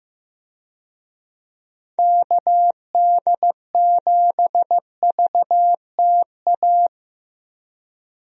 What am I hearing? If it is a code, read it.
Morse code, heard as a keyed tone, about 16 wpm, 700 Hz. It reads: KD7VTA
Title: USA Call Signs - 6 Characters - 15wpm